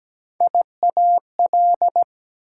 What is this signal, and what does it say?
Morse code, heard as a keyed tone, about 17 wpm, 700 Hz. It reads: IAL
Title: Prefixes and Suffixes 17wpm